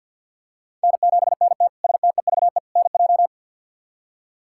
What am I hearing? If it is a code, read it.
Morse code, heard as a keyed tone, about 48 wpm, 700 Hz. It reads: N7NT STEVE NJ
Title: K1USN SST QSO Practice - 50wpm